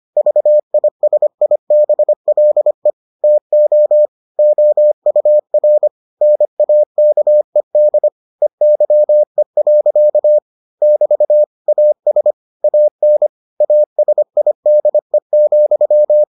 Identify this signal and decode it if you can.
Morse code, heard as a keyed tone, about 25 wpm, 600 Hz. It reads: VISIBLE TO OUR NAKED EYE. <BT> AS AN ASIDE,